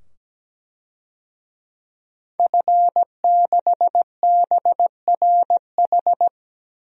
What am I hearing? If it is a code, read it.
Morse code, heard as a keyed tone, about 17 wpm, 700 Hz. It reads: F6BRH
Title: DXCC-100 Call Signs - 5 Characters - 17wpm